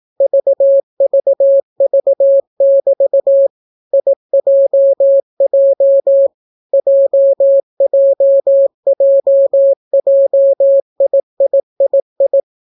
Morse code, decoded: VVV<BT> IJJ JJJJIIII

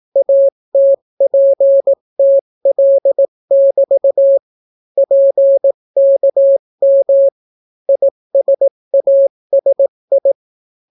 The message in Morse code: ATPTL<BT> PKM ISASI

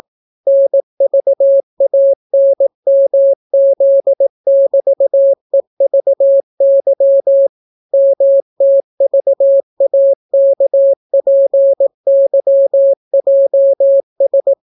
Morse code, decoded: NVANMZ<BT>EVY MTVAKPYJS

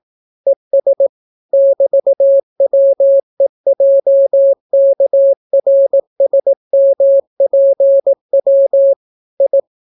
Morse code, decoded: ES <BT>WEJKRSMPW I

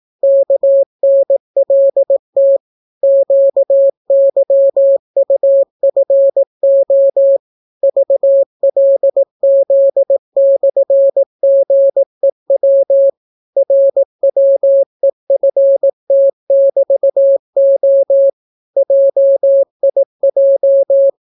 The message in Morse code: KNLT QYUFO VLZ/GEW RWEFT<BT>O JIJ

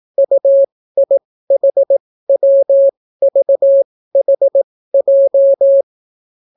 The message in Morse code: UIHWVHJ

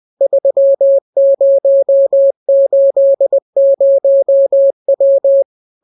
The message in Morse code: 3080W

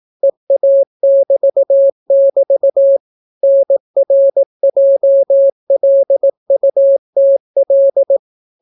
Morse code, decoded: EA<BT><BT> NRJLUTL

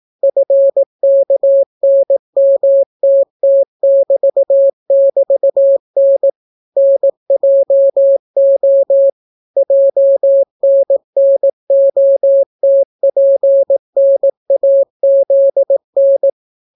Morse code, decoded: FKNMTT<BT><BT>N NJO JNNOTPNAZN

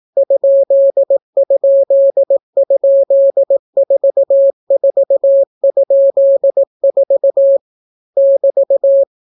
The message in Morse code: ???44?4 <BT>

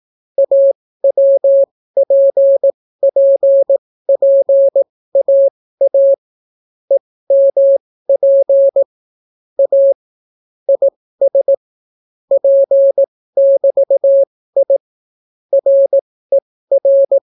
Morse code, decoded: AWPPPAA EMP A IS P<BT>I RER

